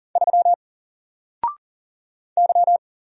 Morse code, heard as a keyed tone, about 40 wpm, 700 Hz. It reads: 3 Y
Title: Single Letter-Number Alt 40wpm